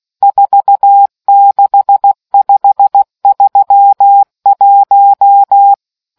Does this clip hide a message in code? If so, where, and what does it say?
Yes: Morse code from 0.2 to 5.7 s, heard as a keyed tone, about 16 wpm, 800 Hz. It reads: 46531